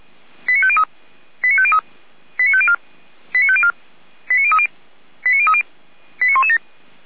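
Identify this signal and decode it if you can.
Five-tone selective call (ZVEI), heard as quick runs of five tones, tones of about 70 ms. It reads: 89503 89503 89504 89504 89030 89030 89117
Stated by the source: ZVEI five-tone selective call 89050 received by SDR.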